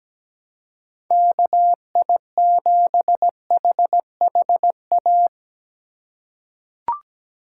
Morse code, heard as a keyed tone, about 17 wpm, 700 Hz. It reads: KI7HHA
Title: USA Call Signs - 2x Character Spacing - 17wpm